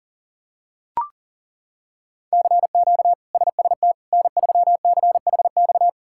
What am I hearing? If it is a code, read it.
Morse code, heard as a keyed tone, about 40 wpm, 700 Hz. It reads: CQ SST N3CHX
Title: K1USN SST QSO Practice - 40wpm